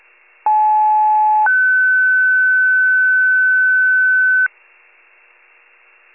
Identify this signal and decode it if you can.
Two-tone page, heard as a one-second tone, then a three-second tone, about 860 then 1500 Hz. It is heard from 0.5 to 4.5 s.